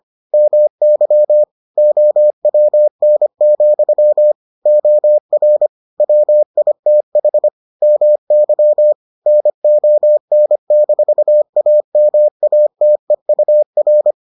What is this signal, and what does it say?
Morse code, heard as a keyed tone, about 25 wpm, 600 Hz. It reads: MY OWN, OR WITH MY NON-AMATEUR